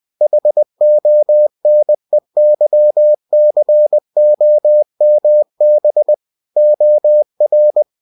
Morse code, heard as a keyed tone, about 20 wpm, 600 Hz. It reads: HONEYCOMB OR